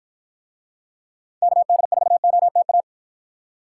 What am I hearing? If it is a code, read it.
Morse code, heard as a keyed tone, about 53 wpm, 700 Hz. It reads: KD4OTA